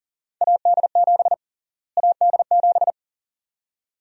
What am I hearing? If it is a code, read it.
Morse code, heard as a keyed tone, about 40 wpm, 700 Hz. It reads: AD7 AD7
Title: USA Call Sign Prefixes - Repeated 2x - 40wpm